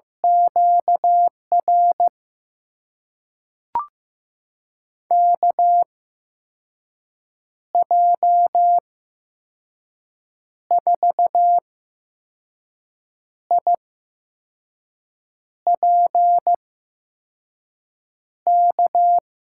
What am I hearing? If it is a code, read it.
Morse code, heard as a keyed tone, about 16 wpm, spaced as for 5 wpm, 700 Hz. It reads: QR KJ4IPK
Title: USA Call Signs - 6 Characters - 8x Character Spacing - 15wpm